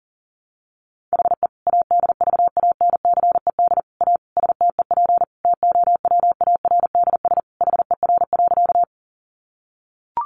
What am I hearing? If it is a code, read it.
Morse code, heard as a keyed tone, about 40 wpm, 700 Hz. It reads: HE ADVANCED A STEP TOWARDS HER.
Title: Sentences from Top 1200 Words 40wpm